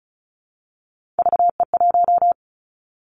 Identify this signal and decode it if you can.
Morse code, heard as a keyed tone, about 36 wpm, 700 Hz. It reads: VE1
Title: Callsign Prefixes for Top 100 DXCC - 35wpm